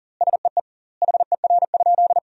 Morse code, heard as a keyed tone, about 40 wpm, 700 Hz. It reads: SEE HER?
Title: 2x Word-Spacing - Sentences from Top 100 Words - 40wpm